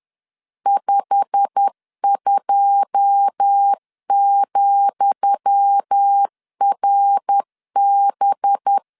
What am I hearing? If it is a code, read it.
Morse code, heard as a keyed tone, about 10 wpm, 800 Hz. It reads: 52,RB
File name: LEZIONI / 10 SIGNS / 10D COMA 10wpm 3013